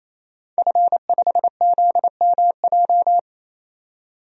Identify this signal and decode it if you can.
Morse code, heard as a keyed tone, about 27 wpm, 690 Hz. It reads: F5ZMJ
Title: DXCC-100 Call Signs - 5 Characters - 28wpm